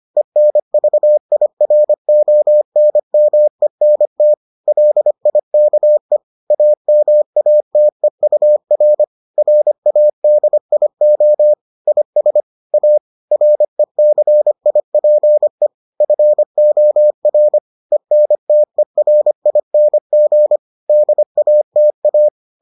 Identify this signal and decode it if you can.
Morse code, heard as a keyed tone, about 25 wpm, 600 Hz. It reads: ENVIRONMENT LIKE AMATEUR RADIO IS A RECIPE FOR ENTERING DATA